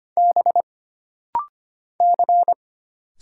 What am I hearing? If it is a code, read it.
Morse code, heard as a keyed tone, about 25 wpm, 700 Hz. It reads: B C
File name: Single Character (Letter:Number:Punctuation:Prosign) - Rapid-Fire - 25wpm